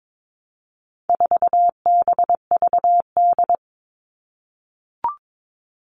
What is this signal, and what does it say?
Morse code, heard as a keyed tone, about 22 wpm, 700 Hz. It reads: 4BVD
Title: All Cut Numbers - 4-Digit 22wpm